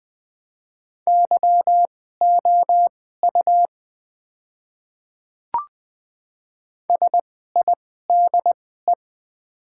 Morse code, heard as a keyed tone, about 20 wpm, spaced as for 14 wpm, 700 Hz. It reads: YOU SIDE
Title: 2x Character Spacing - Top 100 Words - 20wpm